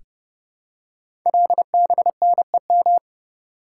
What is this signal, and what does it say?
Morse code, heard as a keyed tone, about 30 wpm, 700 Hz. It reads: LBNEM
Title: Code Groups - 5 Letters - 30wpm